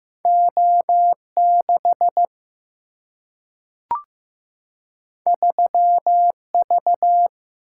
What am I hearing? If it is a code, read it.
Morse code, heard as a keyed tone, about 15 wpm, 700 Hz. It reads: O6 3V
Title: Single-Letter Single-Number 15wpm